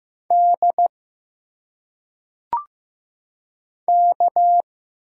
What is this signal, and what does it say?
Morse code, heard as a keyed tone, about 15 wpm, 700 Hz. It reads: D K
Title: Single Letters - Phonetic Pronunciation 15wpm